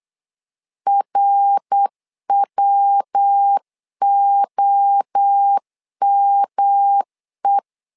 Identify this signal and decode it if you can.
Morse code, heard as a keyed tone, about 8 wpm, 800 Hz. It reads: RWOME
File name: LEZIONI / 4 Q1S3 8wpm 3037